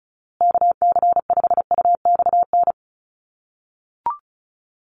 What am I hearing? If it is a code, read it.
Morse code, heard as a keyed tone, about 34 wpm, 700 Hz. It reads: KC5UXN
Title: USA Call Signs Repeated 2x 35wpm